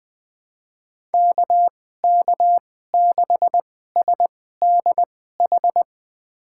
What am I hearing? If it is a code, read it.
Morse code, heard as a keyed tone, about 20 wpm, spaced as for 14 wpm, 700 Hz. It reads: KK6SDH
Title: USA Call Signs - 6 Characters - 2x Character Spacing - 20wpm